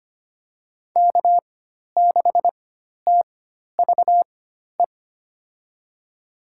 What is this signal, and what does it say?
Morse code, heard as a keyed tone, about 25 wpm, spaced as for 12 wpm, 700 Hz. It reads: K6TVE
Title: USA Call Signs - 4x Character Spacing - 25wpm